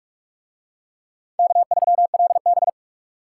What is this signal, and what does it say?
Morse code, heard as a keyed tone, about 47 wpm, 700 Hz. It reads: K3LB